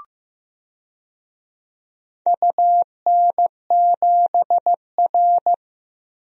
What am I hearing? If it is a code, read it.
Morse code, heard as a keyed tone, about 14 wpm, 700 Hz. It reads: UN7R
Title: DXCC-100 Call Signs - 4 Characters - Repeated 2x - 15wpm